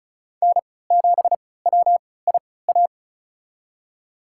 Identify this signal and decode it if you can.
Morse code, heard as a keyed tone, about 36 wpm, spaced as for 20 wpm, 700 Hz. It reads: N7WIA
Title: USA Call Signs - 5 Characters - 3x Character Spacing - 35wpm